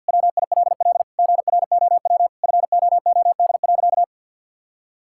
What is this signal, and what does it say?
Morse code, heard as a keyed tone, about 50 wpm, 700 Hz. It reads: WILL GROW FOOD.
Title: Sentences from Top 1000 Words - Review Part 6 - 50wpm